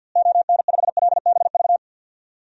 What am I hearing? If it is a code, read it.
Morse code, heard as a keyed tone, about 50 wpm, 700 Hz. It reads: ON5LBV
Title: DXCC-100 Call Signs - 6 Characters - Repeated 2x - 50wpm